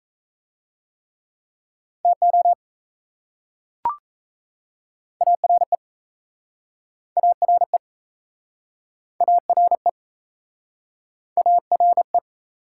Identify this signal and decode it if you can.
Morse code, heard as a keyed tone, about 38 wpm, 700 Hz. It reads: TO ARE ARE ARE ARE